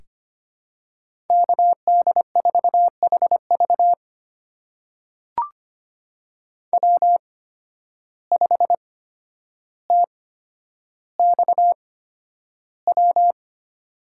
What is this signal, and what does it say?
Morse code, heard as a keyed tone, about 25 wpm, spaced as for 11 wpm, 700 Hz. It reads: KD4HV W5TXW
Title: USA Call Signs - 5 Characters - 8x Character Spacing - 25wpm